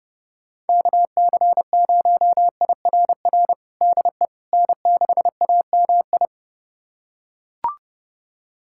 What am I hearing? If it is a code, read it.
Morse code, heard as a keyed tone, about 31 wpm, 700 Hz. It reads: KC0IRR DE N6AMI
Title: USA Callsign DE Callsign 30wpm